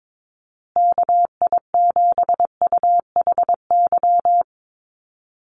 Morse code, heard as a keyed tone, about 22 wpm, 700 Hz. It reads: KI7UHY
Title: USA Call Signs Repeated 2x 22wpm